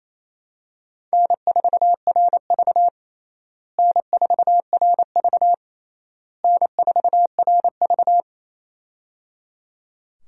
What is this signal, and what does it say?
Morse code, heard as a keyed tone, about 28 wpm, 700 Hz. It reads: N4RV N4RV N4RV
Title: USA Call Signs - 4 Characters - Repeated 3x - 28wpm